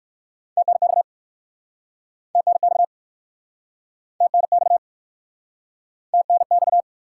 Morse code, heard as a keyed tone, about 56 wpm, 700 Hz. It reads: TNX TNX TNX TNX
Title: Speed-Racing Top 90 QSO Elements 45wpm